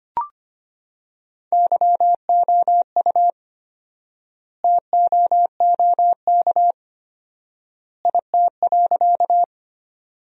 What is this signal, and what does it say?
Morse code, heard as a keyed tone, about 25 wpm, 700 Hz. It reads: YOU TOOK IT.
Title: 4x Word-Spacing - Sentences from Top 300 Words - 25wpm